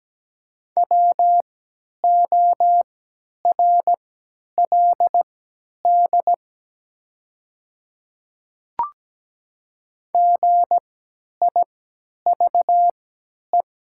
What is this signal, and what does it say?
Morse code, heard as a keyed tone, about 17 wpm, spaced as for 10 wpm, 700 Hz. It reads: WORLD GIVE
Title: 3x Character Spacing - Top 200 Words - 17wpm